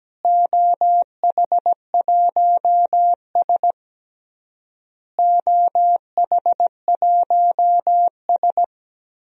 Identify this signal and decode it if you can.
Morse code, heard as a keyed tone, about 17 wpm, 700 Hz. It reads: OH1S OH1S
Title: DXCC-100 Call Signs - 4 Characters - Repeated 3x - 17wpm